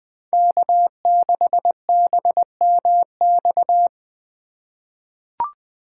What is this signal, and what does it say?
Morse code, heard as a keyed tone, about 20 wpm, 700 Hz. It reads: K6BMX
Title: USA Call Signs - 4x Character Spacing - 20wpm